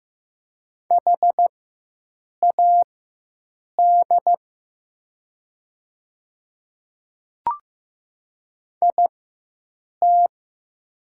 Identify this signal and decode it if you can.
Morse code, heard as a keyed tone, about 15 wpm, spaced as for 7 wpm, 700 Hz. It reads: HAD IT
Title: 4x Character Spacing - Top 100 Words - 15wpm